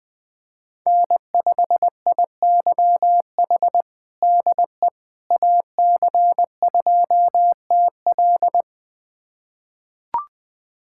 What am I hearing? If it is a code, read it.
Morse code, heard as a keyed tone, about 20 wpm, 700 Hz. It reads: N5IYH DE AC2TL
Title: USA Callsign DE Callsign - 5 Characters - 20wpm